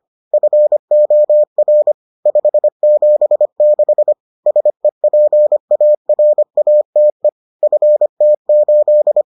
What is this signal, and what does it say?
Morse code, heard as a keyed tone, about 25 wpm, 600 Hz. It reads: FOR 576 SEPARATE FT8